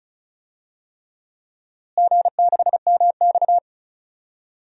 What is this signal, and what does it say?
Morse code, heard as a keyed tone, about 36 wpm, 700 Hz. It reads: G6MX
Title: DXCC-100 Call Signs - 4 Characters - Repeated 3x - 35wpm